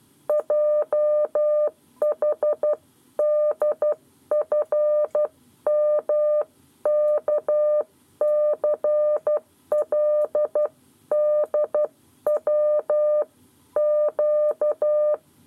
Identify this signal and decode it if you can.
Morse code, heard as a keyed tone, about 12 wpm, 600 Hz. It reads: JHDFMKCLDW Q
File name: 7 Lezione cw 1